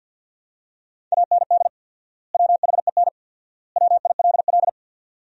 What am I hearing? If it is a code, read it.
Morse code, heard as a keyed tone, about 52 wpm, 700 Hz. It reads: AND WHEN WILL
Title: 4x Word-Spacing - Sentences from Top 100 Words - 50wpm